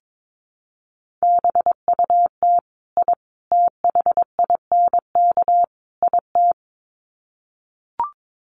Morse code, heard as a keyed tone, about 22 wpm, 700 Hz. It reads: BUT I THINK IT
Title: Sets of 4 Words 22wpm